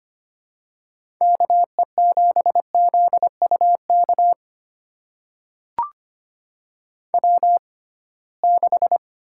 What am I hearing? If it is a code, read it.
Morse code, heard as a keyed tone, about 25 wpm, spaced as for 19 wpm, 700 Hz. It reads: KE7ZUK W6
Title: USA Call Signs - 6x Character Spacing - 25wpm